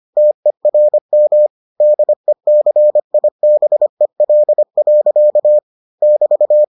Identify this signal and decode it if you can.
Morse code, heard as a keyed tone, about 25 wpm, 600 Hz. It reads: TERM DECIBEL. <BT>